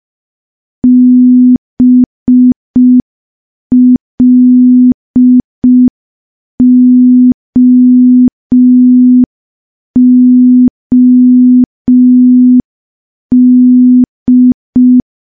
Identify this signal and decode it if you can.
Morse code, heard as a keyed tone, about 5 wpm, 250 Hz. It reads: BLOOD
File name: morse / morse-training-05WPM